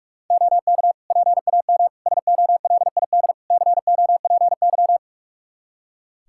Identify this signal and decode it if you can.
Morse code, heard as a keyed tone, about 46 wpm, 700 Hz. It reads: OK PAM SOLID COPY
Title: K1USN SST QSO Practice 45wpm